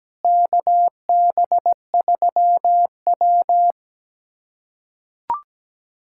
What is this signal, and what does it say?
Morse code, heard as a keyed tone, about 17 wpm, 700 Hz. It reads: KB3W
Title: USA Callsign DE Callsign - 4 Characters - 17wpm